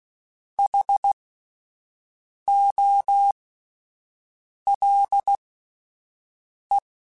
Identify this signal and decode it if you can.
Morse code, heard as a keyed tone, about 16 wpm, spaced as for 5 wpm, 800 Hz. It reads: HOLE
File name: Alphabetical / Alphabetical mp3s / Hs @5